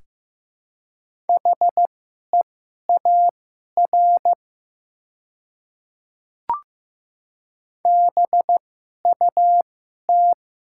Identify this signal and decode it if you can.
Morse code, heard as a keyed tone, about 15 wpm, spaced as for 11 wpm, 700 Hz. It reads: HEAR BUT
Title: 2x Character Spacing - Top 500 Words - 15wpm